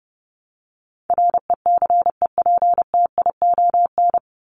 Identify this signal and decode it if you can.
Morse code, heard as a keyed tone, about 30 wpm, 700 Hz. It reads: RECEPTION
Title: Top 1751-2000 Words 30wpm